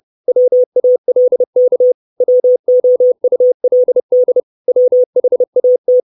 Morse code, heard as a keyed tone, about 30 wpm, 500 Hz. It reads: WALK WOULD WHAT
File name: 632 Words 30WPM